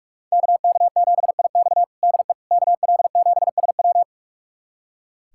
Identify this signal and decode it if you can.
Morse code, heard as a keyed tone, about 46 wpm, 700 Hz. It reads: KK7IX DE KL7SW
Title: USA Callsign DE Callsign - 5 Characters - 45wpm